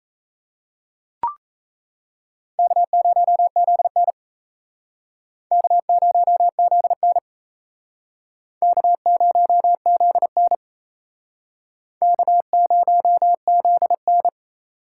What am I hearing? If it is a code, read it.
Morse code, heard as a keyed tone, about 34 wpm, 700 Hz. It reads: K0ZN K0ZN K0ZN K0ZN
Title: Speed Racing - USA 1x2 Call Signs 28wpm